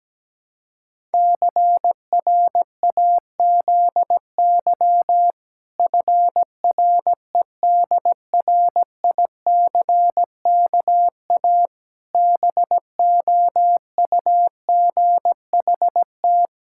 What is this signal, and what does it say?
Morse code, heard as a keyed tone, about 17 wpm, 700 Hz. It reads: CRAZY FREDRICKA BOUGHT
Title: Pangram Challenge v2 - 17wpm